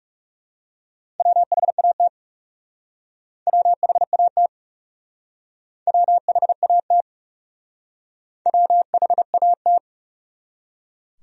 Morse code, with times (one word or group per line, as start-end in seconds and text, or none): 1.2-2.1 WHAT
3.5-4.5 WHAT
5.9-7.0 WHAT
8.5-9.8 WHAT